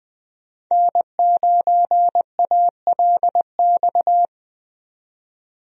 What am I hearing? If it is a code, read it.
Morse code, heard as a keyed tone, about 20 wpm, 700 Hz. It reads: N9ALX
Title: USA Call Signs - 5 Characters - 20wpm